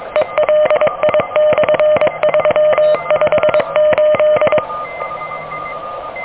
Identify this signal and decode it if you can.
Morse code, heard as a keyed tone, about 23 wpm, 610 Hz, with radio noise. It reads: ELI/358
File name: OK QRP 2023 / OK QRP 2023